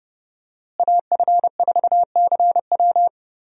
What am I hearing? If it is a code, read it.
Morse code, heard as a keyed tone, about 30 wpm, 700 Hz. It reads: AF4CW
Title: USA Call Signs - 5 Characters - Repeated 2x - 30wpm